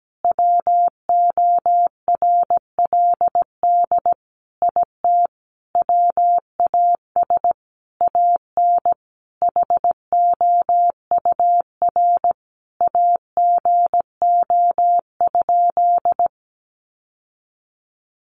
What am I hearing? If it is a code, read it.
Morse code, heard as a keyed tone, about 17 wpm, 700 Hz. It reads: WORLD IT WAS AN HOUR AGO?
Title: Sentences from Top 1300 Words 17wpm